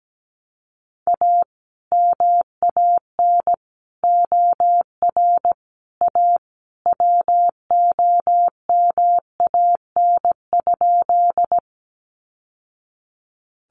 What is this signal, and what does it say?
Morse code, heard as a keyed tone, about 17 wpm, 700 Hz. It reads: A MAN OR A WOMAN?
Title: Sentences from Top 1100 Words 17wpm